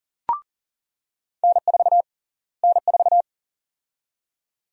Morse code, heard as a keyed tone, about 39 wpm, 700 Hz. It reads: N4 N4
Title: USA Call Sign Prefixes - Repeated 2x - 40wpm